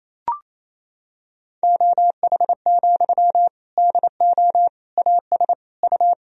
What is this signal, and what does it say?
Morse code, heard as a keyed tone, about 28 wpm, 700 Hz. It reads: OH, DO AS U
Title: Sentences from Top 500 Words and Abbreviations 28wpm